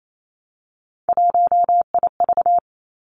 Morse code, heard as a keyed tone, about 28 wpm, 690 Hz. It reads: 1IV
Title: Three Challenging Letter-Number Combinations 28wpm